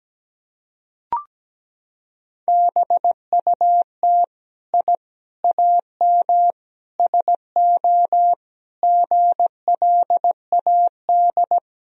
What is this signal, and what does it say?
Morse code, heard as a keyed tone, about 17 wpm, 700 Hz. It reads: BUT I AM SO GLAD
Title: Sentences from Top 1000 Words - Review Part 2 - 17wpm